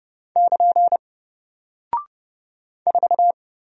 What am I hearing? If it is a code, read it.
Morse code, heard as a keyed tone, about 30 wpm, 700 Hz. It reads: <KN> 4
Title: Single Character (Letter:Number:Punctuation:Prosign) Alt 30wpm